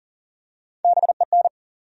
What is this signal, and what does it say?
Morse code, heard as a keyed tone, about 40 wpm, 700 Hz. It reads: BEN
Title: USA Common Names - Max 4 Letters - 40wpm